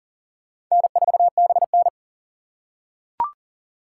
Morse code, heard as a keyed tone, about 39 wpm, 700 Hz. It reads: N4BN
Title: USA Call Signs - 4 Characters - 6x Character Spacing - 40wpm